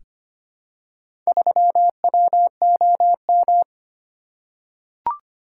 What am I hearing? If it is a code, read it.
Morse code, heard as a keyed tone, about 25 wpm, 700 Hz. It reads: 3WOM
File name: practice / 25wpm / Four Letter-Number Combinations - 25wpm